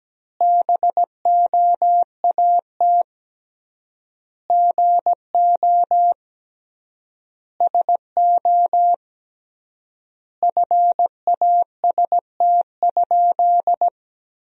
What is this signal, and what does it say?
Morse code, heard as a keyed tone, about 17 wpm, 700 Hz. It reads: BOAT GO SO FAST?
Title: 3x Word-Spacing - Sentences from Top 500 Words - 17wpm